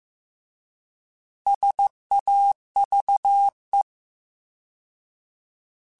Morse code, heard as a keyed tone, about 15 wpm, 800 Hz. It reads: SAVE
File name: Alphabetical / Alphabetical mp3s / Ss @13